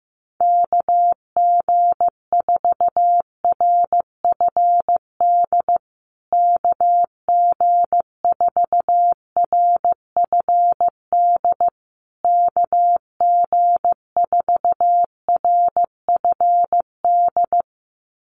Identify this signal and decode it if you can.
Morse code, heard as a keyed tone, about 15 wpm, 700 Hz. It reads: KG4RFD KG4RFD KG4RFD
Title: USA Call Signs Repeated 3x 15wpm